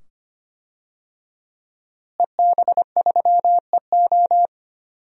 Morse code, heard as a keyed tone, about 25 wpm, 700 Hz. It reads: EB3EO
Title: DXCC-100 Call Signs - 5 Characters - Repeated 2x - 25wpm